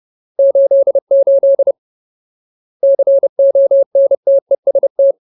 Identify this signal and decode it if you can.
Morse code, heard as a keyed tone, about 30 wpm, 550 Hz. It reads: 88 CONTEST
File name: Ultimate Word List / 30WPM Ultimate